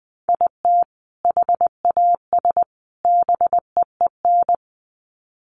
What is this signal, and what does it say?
Morse code, heard as a keyed tone, about 20 wpm, 700 Hz. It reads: IT HAS BEEN
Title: Sets of 3 Words 20wpm